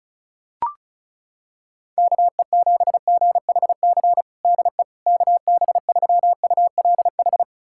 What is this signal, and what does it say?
Morse code, heard as a keyed tone, about 34 wpm, 700 Hz. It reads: KE7GHC DE KB3ULH
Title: USA Callsign DE Callsign - 6 Characters - 35wpm